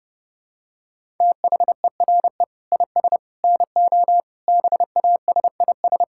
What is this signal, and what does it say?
Morse code, heard as a keyed tone, about 32 wpm, 700 Hz. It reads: THERE IS NO BASIS
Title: Sentences from Top 1000 Words - Review Part 5 - 30wpm